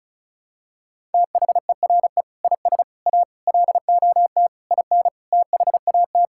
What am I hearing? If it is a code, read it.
Morse code, heard as a keyed tone, about 36 wpm, 700 Hz. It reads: THERE IS A LOT IN THAT
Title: Sentences from Top 1000 Words - Review Part 5 - 35wpm